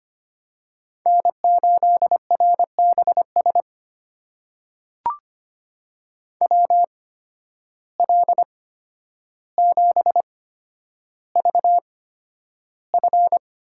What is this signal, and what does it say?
Morse code, heard as a keyed tone, about 25 wpm, spaced as for 11 wpm, 700 Hz. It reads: N8RBS WL7VF
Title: USA Call Signs - 5 Characters - 8x Character Spacing - 25wpm